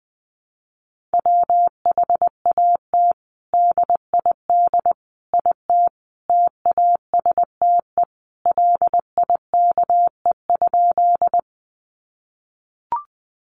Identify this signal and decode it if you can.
Morse code, heard as a keyed tone, about 20 wpm, 700 Hz. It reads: WHAT DID IT TASTE LIKE?
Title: Sentences from Top 1100 Words 20wpm